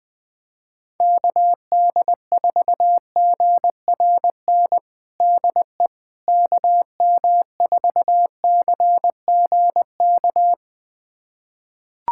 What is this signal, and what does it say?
Morse code, heard as a keyed tone, about 20 wpm, 700 Hz. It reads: KD4GRN DE KM4CGK
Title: USA Callsign DE Callsign - 6 Characters - 20wpm